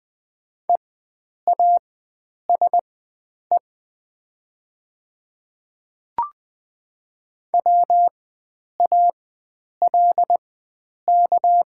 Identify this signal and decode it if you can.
Morse code, heard as a keyed tone, about 20 wpm, spaced as for 9 wpm, 690 Hz. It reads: EASE WALK
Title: 4x Character Spacing - Top 300 Words - 20wpm